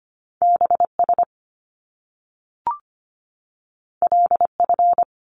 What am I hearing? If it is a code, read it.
Morse code, heard as a keyed tone, about 25 wpm, 700 Hz. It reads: BS LF
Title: Top Two Letter Combinations in Words 25wpm